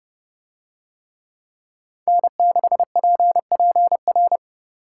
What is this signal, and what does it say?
Morse code, heard as a keyed tone, about 28 wpm, 700 Hz. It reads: N6PPR